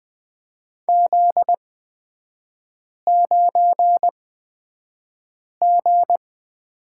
Morse code, heard as a keyed tone, about 20 wpm, spaced as for 5 wpm, 700 Hz. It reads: Z9G
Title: 243 Lesson 39e - Letter z - Three Characters 20wpm